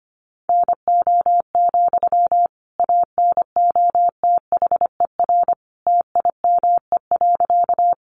Morse code, heard as a keyed tone, about 25 wpm, 700 Hz. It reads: NO, ANOTHER TIME.
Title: Sentences from Top 600 Words 25wpm